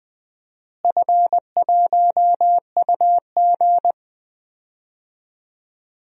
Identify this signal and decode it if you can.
Morse code, heard as a keyed tone, about 20 wpm, 700 Hz. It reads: F1UG
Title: DXCC-100 Call Signs - 4 Characters - Repeated 3x - 20wpm